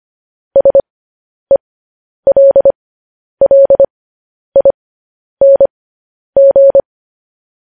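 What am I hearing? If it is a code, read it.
Morse code, heard as a keyed tone, about 25 wpm, spaced as for 10 wpm, 550 Hz. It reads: SELLING
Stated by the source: Morse code news headlines practice at 10 wpm.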